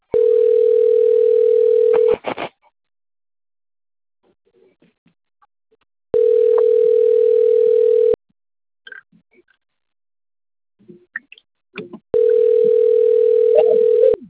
Ringback tone (North America), 440 + 480 Hz.
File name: out- - - - - .14103